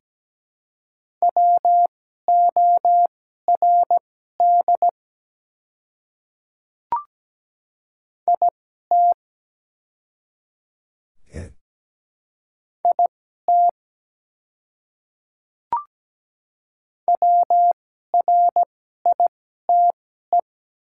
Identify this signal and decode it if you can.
Morse code, heard as a keyed tone, about 17 wpm, spaced as for 12 wpm, 700 Hz. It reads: WORD IT IT WRITE
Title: 2x Character Spacing - Top 100 Words - 17wpm